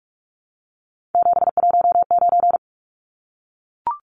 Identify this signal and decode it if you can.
Morse code, heard as a keyed tone, about 44 wpm, 700 Hz. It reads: 719
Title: Numbers - 3-Digit 45wpm